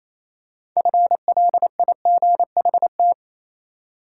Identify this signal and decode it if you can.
Morse code, heard as a keyed tone, about 28 wpm, 700 Hz. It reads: FLIGHT
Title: Six Letter Words - 28wpm